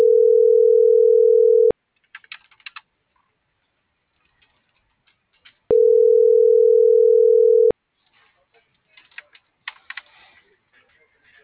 Ringback tone (North America), 440 + 480 Hz.